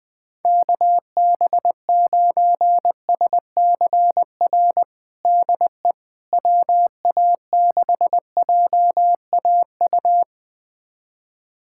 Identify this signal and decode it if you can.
Morse code, heard as a keyed tone, about 20 wpm, 700 Hz. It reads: KB9SCR DE WA6JAU
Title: USA Callsign DE Callsign - 6 Characters - 20wpm